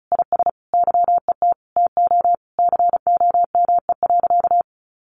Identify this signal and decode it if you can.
Morse code, heard as a keyed tone, about 34 wpm, 700 Hz. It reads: IS YET TO COME.
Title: Sentences from Top 1500 Words 35wpm